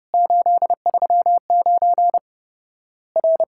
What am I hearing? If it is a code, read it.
Morse code, heard as a keyed tone, about 30 wpm, 700 Hz. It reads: 839 R
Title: Worked All Europe DX Contest - QTC Exchange - 30wpm